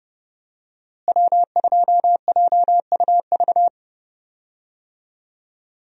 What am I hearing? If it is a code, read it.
Morse code, heard as a keyed tone, about 30 wpm, 700 Hz. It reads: W2JUV
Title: USA Call Signs - 5 Characters - Repeated 2x - 30wpm